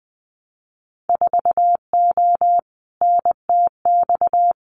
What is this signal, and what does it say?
Morse code, heard as a keyed tone, about 20 wpm, 700 Hz. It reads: 4O NTX
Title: Winter Field Day Exchanges 20wpm